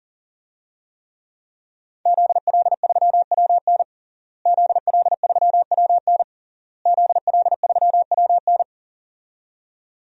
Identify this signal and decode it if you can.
Morse code, heard as a keyed tone, about 40 wpm, 700 Hz. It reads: ZL3WN ZL3WN ZL3WN
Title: DXCC-100 Call Signs - 5 Characters - Repeated 3x - 40wpm